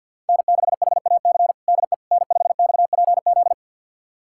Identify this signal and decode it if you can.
Morse code, heard as a keyed tone, about 48 wpm, 700 Hz. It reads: N6HAC DE N5XPZ
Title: USA Callsign DE Callsign - 5 Characters - 50wpm